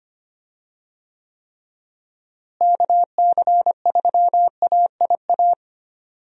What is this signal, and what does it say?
Morse code, heard as a keyed tone, about 25 wpm, 700 Hz. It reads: KC3AIA
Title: USA Call Signs - 6 Characters - Repeated 3x - 25wpm